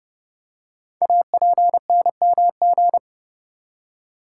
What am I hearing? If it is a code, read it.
Morse code, heard as a keyed tone, about 30 wpm, 700 Hz. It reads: APNMG
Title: Code Groups - 5 Letters - 30wpm